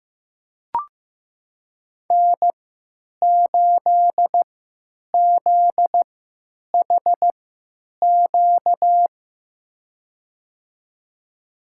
Morse code, heard as a keyed tone, about 15 wpm, spaced as for 9 wpm, 700 Hz. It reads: N8ZHQ
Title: USA Call Signs - 5 Characters - 3x Character Spacing - 15wpm